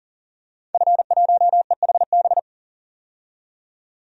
Morse code, heard as a keyed tone, about 40 wpm, 700 Hz. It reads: F1EHB
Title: DXCC-100 Call Signs - 5 Characters - 40wpm